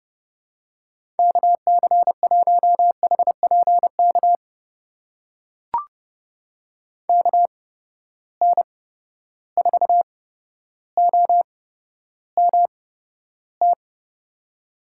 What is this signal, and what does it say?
Morse code, heard as a keyed tone, about 32 wpm, spaced as for 13 wpm, 700 Hz. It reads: KC1HPK KN4OMT
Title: USA Call Signs - 6 Characters - 8x Character Spacing - 30wpm